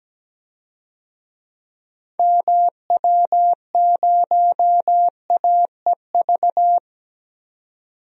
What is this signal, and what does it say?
Morse code, heard as a keyed tone, about 17 wpm, 700 Hz. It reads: MW0AEV
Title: DXCC-100 Call Signs - 6 Characters - 17wpm